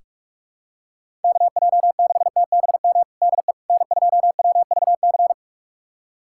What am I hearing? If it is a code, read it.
Morse code, heard as a keyed tone, about 46 wpm, 700 Hz. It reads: KJ6TBM DE N2WVC
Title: USA Callsign DE Callsign 45wpm